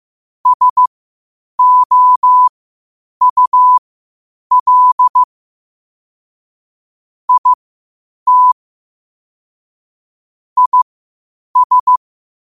Morse code, heard as a keyed tone, about 15 wpm, spaced as for 8 wpm, 1000 Hz. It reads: SOUL IT IS
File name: qotdfiles / 8wpm